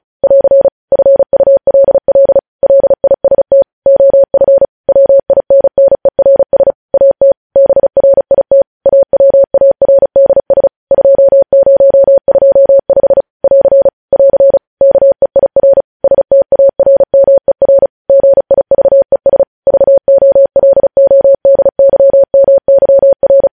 Morse code, heard as a keyed tone, about 36 wpm, 550 Hz. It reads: <AR> FULL LIST OF WINNERS AT BRIT AWARDS 2025 <AR> <AR> KEIR STARMER GIVES VOLODYMYR